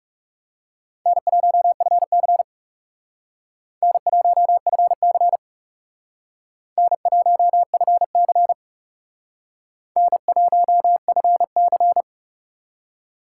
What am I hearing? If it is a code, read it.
Morse code, heard as a keyed tone, about 40 wpm, 700 Hz. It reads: N1FC N1FC N1FC N1FC